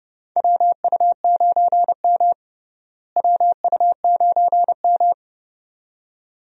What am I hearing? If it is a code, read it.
Morse code, heard as a keyed tone, about 30 wpm, 700 Hz. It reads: WU9M WU9M
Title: USA Call Signs - 4 Characters - Repeated 2x - 30wpm